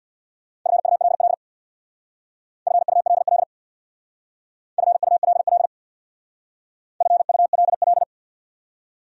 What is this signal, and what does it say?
Morse code, heard as a keyed tone, about 64 wpm, 700 Hz. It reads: FULL FULL FULL FULL